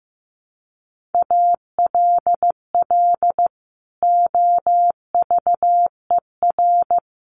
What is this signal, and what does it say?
Morse code, heard as a keyed tone, about 15 wpm, 700 Hz. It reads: ALL OVER
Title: Sets of 3 Words 15wpm